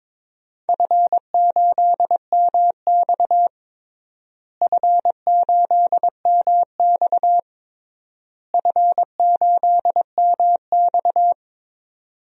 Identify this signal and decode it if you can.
Morse code, heard as a keyed tone, about 22 wpm, 700 Hz. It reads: F8MX F8MX F8MX
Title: DXCC-100 Call Signs - 4 Characters - Repeated 3x - 22wpm